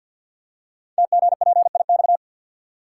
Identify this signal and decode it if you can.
Morse code, heard as a keyed tone, about 50 wpm, 690 Hz. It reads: TGPIX
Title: Code Groups - 5 Letters - 50wpm